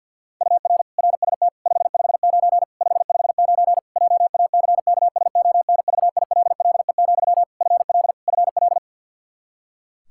Morse code, heard as a keyed tone, about 51 wpm, 700 Hz. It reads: UR RST 559 559 JACKSONVILLE, FL FL
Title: Practice QSO - Exchange 1 - Beginning - 50wpm